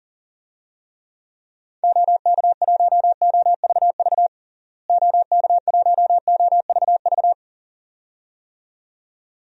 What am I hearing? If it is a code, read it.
Morse code, heard as a keyed tone, about 40 wpm, 700 Hz. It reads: OK1OVV OK1OVV
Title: DXCC-100 Call Signs - 6 Characters - Repeated 2x - 40wpm